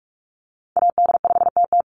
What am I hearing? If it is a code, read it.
Morse code, heard as a keyed tone, about 44 wpm, 700 Hz. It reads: AD5TT
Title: Call Signs - 45wpm